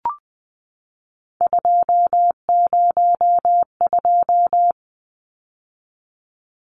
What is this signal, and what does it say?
Morse code, heard as a keyed tone, about 20 wpm, 700 Hz. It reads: 202
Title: Numbers - 3-Digit 20wpm